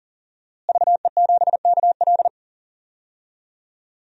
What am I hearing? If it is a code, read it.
Morse code, heard as a keyed tone, about 40 wpm, 700 Hz. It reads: VE7KL
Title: DXCC-100 Call Signs - 5 Characters - Repeated 3x - 40wpm